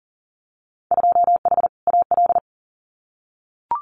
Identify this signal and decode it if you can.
Morse code, heard as a keyed tone, about 41 wpm, 700 Hz. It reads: 2H AL
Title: Winter Field Day Exchanges 40wpm